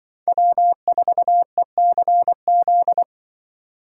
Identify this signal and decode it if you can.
Morse code, heard as a keyed tone, about 25 wpm, 700 Hz. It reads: W4ECZ